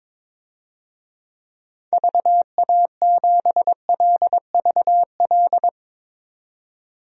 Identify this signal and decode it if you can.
Morse code, heard as a keyed tone, about 22 wpm, 700 Hz. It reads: VA7LVL
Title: DXCC-100 Call Signs - 6 Characters - 22wpm